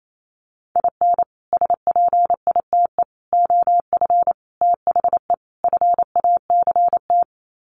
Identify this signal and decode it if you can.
Morse code, heard as a keyed tone, about 28 wpm, 700 Hz. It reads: IN SPITE OF THE FACT
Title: Sets of 5 Words Encore2 28wpm